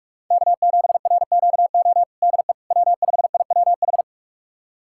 Morse code, heard as a keyed tone, about 41 wpm, 700 Hz. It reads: K7RQO DE W5IWH
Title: USA Callsign DE Callsign - 5 Characters - 45wpm